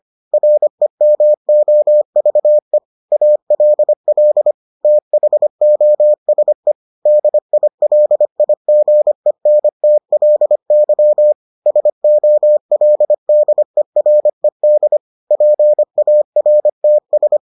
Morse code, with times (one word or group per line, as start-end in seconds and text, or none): 0.3-2.8 REMOVE
3.1-4.5 ALL
4.8-6.7 THOSE
7.1-11.3 DILIGENTLY
11.7-15.0 SOLDERED
15.3-17.4 PARTS